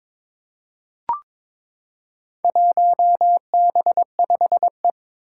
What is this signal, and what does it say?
Morse code, heard as a keyed tone, about 22 wpm, 700 Hz. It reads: 1B5E
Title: Four Challenging Letter-Number Combinations 22wpm